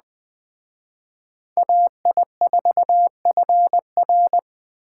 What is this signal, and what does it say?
Morse code, heard as a keyed tone, about 20 wpm, 700 Hz. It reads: AI4FR
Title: Speed Racing - USA Call Signs 17wpm